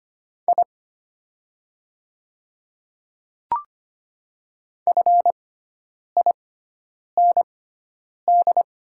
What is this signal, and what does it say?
Morse code, heard as a keyed tone, about 25 wpm, spaced as for 9 wpm, 690 Hz. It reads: I FIND
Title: 6x Character Spacing - Top 100 Words - 25wpm